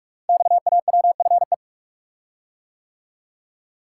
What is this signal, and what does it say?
Morse code, heard as a keyed tone, about 44 wpm, 700 Hz. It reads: XAWFE
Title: Code Groups - 5 Letters - 45wpm